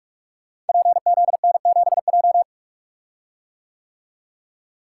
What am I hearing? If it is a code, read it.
Morse code, heard as a keyed tone, about 43 wpm, 700 Hz. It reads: PZN7J
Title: Code Groups - 5 Letter:Numbers - 45wpm